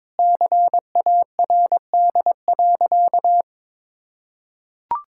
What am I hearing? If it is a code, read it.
Morse code, heard as a keyed tone, about 22 wpm, 700 Hz. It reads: CARD.
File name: Sentences with 7-Word Phrases - 22wpm